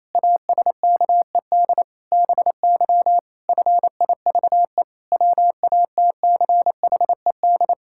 Morse code, heard as a keyed tone, about 28 wpm, 700 Hz. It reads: ASKED BY FIVE WATCHED
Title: Pangram Challenge v2 - 28wpm